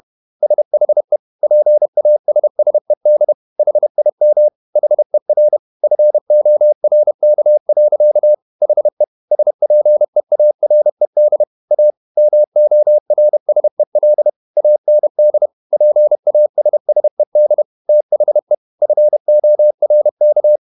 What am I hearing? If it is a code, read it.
Morse code, heard as a keyed tone, about 31 wpm, 600 Hz. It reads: SHE PASSED HIM HER FORK. HE SPEARED A MORSEL AND PASSED THE FORK
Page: Never 31wpm | CW med Gnister